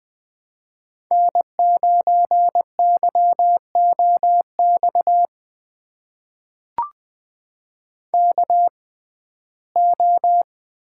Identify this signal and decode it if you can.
Morse code, heard as a keyed tone, about 20 wpm, spaced as for 15 wpm, 700 Hz. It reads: N9YOX KO
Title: USA Call Signs - 6x Character Spacing - 20wpm